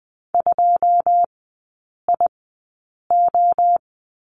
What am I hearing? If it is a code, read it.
Morse code, heard as a keyed tone, about 20 wpm, spaced as for 8 wpm, 700 Hz. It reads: 2IO
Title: 121 Lesson 20e - Number 3 - Three Characters 20wpm